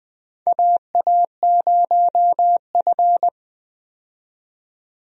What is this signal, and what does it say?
Morse code, heard as a keyed tone, about 21 wpm, 700 Hz. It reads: AA0F
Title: USA Call Signs - 4 Characters - 20wpm